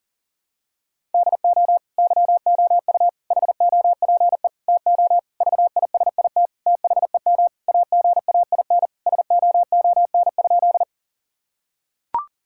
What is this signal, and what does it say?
Morse code, heard as a keyed tone, about 40 wpm, 690 Hz. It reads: DO YOU HOPE TO VISIT THEM AGAIN SOON?
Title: Sentences from Top 1000 Words - Review Part 6 - 40wpm